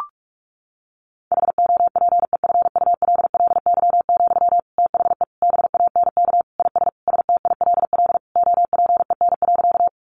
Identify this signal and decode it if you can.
Morse code, heard as a keyed tone, about 44 wpm, 700 Hz. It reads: HOPEFULLY, THE BANK IS STILL OPEN.
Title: Sentences from Top 1500 Words 45wpm